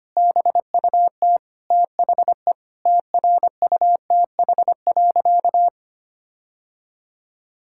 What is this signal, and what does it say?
Morse code, heard as a keyed tone, about 25 wpm, 700 Hz. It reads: BUT THE TRUTH.
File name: practice / 25wpm / Sentences with 9-Word Phrases - 25wpm